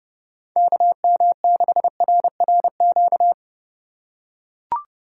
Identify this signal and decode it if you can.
Morse code, heard as a keyed tone, about 28 wpm, 700 Hz. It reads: KM6RRQ